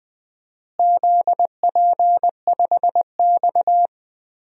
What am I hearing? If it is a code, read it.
Morse code, heard as a keyed tone, about 20 wpm, 700 Hz. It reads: ZP5X
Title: DXCC-100 Call Signs - 4 Characters - Repeated 2x - 20wpm